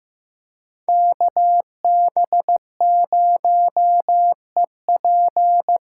Morse code, heard as a keyed tone, about 18 wpm, 700 Hz. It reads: KB0EP
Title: Speed Racing - USA 2x2 Call Signs 15wpm